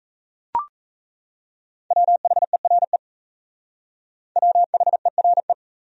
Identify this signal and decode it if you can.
Morse code, heard as a keyed tone, about 35 wpm, 700 Hz. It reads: WHERE WHERE
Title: Speed-Racing Top 200 Words 28wpm